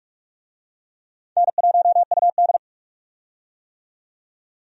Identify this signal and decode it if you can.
Morse code, heard as a keyed tone, about 43 wpm, 700 Hz. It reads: N1UD
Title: USA Call Signs - 4 Characters - 45wpm